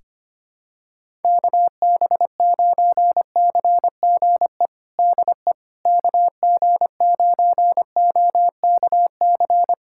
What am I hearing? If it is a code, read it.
Morse code, heard as a keyed tone, about 25 wpm, 700 Hz. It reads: KB9CGE DE KG9OKC
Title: USA Callsign DE Callsign - 6 Characters - 25wpm